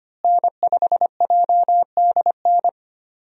Morse code, heard as a keyed tone, about 25 wpm, 700 Hz. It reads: N5JDN
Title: USA Call Signs - 5 Characters - Repeated 3x - 25wpm